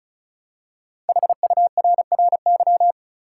Morse code, heard as a keyed tone, about 36 wpm, 700 Hz. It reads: HURRY
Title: Five Letter Words - 35wpm